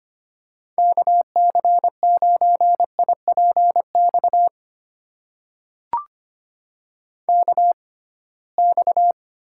Morse code, heard as a keyed tone, about 25 wpm, spaced as for 19 wpm, 700 Hz. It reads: KC9IPX KX
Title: USA Call Signs - 6x Character Spacing - 25wpm